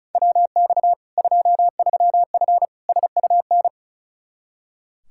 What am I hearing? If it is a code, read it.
Morse code, heard as a keyed tone, about 34 wpm, 700 Hz. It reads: WX 23F SUN
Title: Practice QSO - Exchange 2 - Info - 35wpm